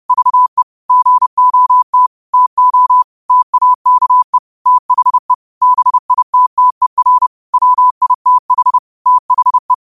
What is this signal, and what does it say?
Morse code, heard as a keyed tone, about 30 wpm, 1000 Hz. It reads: VE GOT TO TAKE THE BITTER WITH THE
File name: qotdfiles / 30wpm